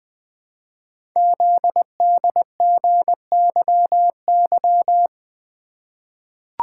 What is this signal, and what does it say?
Morse code, heard as a keyed tone, about 20 wpm, 700 Hz. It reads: ZDGYY
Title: Code Groups - 5 Letters - 20wpm